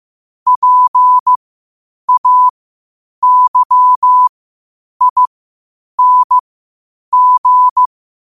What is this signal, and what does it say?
Morse code, heard as a keyed tone, about 15 wpm, spaced as for 8 wpm, 1000 Hz. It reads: PAYING
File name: qotdfiles / 8wpm